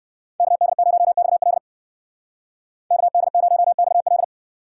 Morse code, heard as a keyed tone, about 54 wpm, 700 Hz. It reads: KD0XL KD0XL
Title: Speed Racing - USA 2x2 Call Signs 45wpm